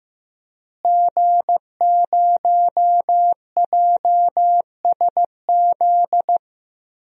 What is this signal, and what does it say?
Morse code, heard as a keyed tone, about 15 wpm, 700 Hz. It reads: G0JSZ
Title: DXCC-100 Call Signs - 5 Characters - Repeated 2x - 15wpm